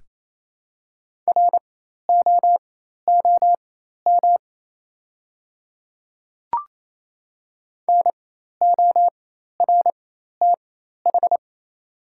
Morse code, heard as a keyed tone, about 28 wpm, spaced as for 13 wpm, 700 Hz. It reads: ROOM NORTH
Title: 4x Character Spacing - Top 300 Words - 28wpm